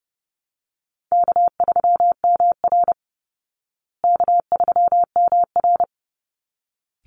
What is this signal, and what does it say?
Morse code, heard as a keyed tone, about 30 wpm, 700 Hz. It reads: K3MR K3MR
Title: Call Signs Repeated 3x - 30wpm